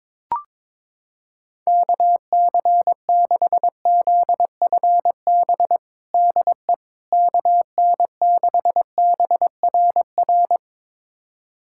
Morse code, heard as a keyed tone, about 22 wpm, 700 Hz. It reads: KC6ZFB DE KN6BRR
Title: USA Callsign DE Callsign 22wpm